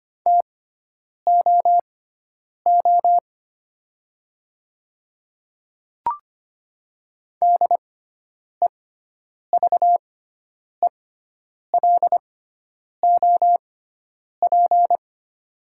Morse code, heard as a keyed tone, about 25 wpm, spaced as for 9 wpm, 690 Hz. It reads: TOO DEVELOP
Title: 6x Character Spacing - Top 500 Words - 25wpm